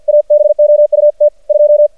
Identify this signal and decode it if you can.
Morse code, heard as a keyed tone, about 50 wpm, 590 Hz. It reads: ABOUT 2
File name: archived / CRSnet